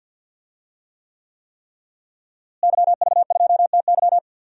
Morse code, heard as a keyed tone, about 51 wpm, 700 Hz. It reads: YV2TY
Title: DXCC-100 Call Signs - 5 Characters - Repeated 3x - 50wpm